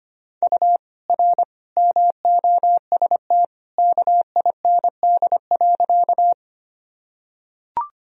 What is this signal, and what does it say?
Morse code, heard as a keyed tone, about 25 wpm, 700 Hz. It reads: U R MOST KIND.
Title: Sentences from Top 200 Words and Abbreviations 25wpm